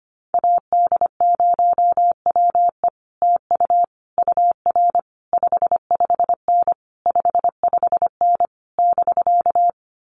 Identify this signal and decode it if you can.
Morse code, heard as a keyed tone, about 25 wpm, 700 Hz. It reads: AD0WE TU UR 55N 55N <BK>
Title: Common QSO Phrases Encore 25wpm